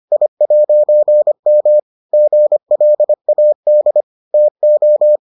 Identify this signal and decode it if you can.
Morse code, heard as a keyed tone, about 25 wpm, 600 Hz. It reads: I'M GLAD TO